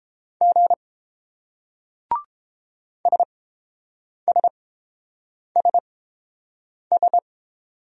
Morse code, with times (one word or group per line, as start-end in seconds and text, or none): 0.4-0.7 G
3.1-7.2 SSSS